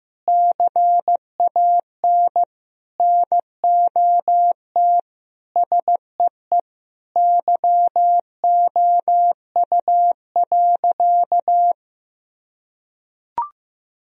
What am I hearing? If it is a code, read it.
Morse code, heard as a keyed tone, about 14 wpm, 700 Hz. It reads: CAN NOT SEE YOU.
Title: Sentences from Top 1000 Words - Review Part 3 - 15wpm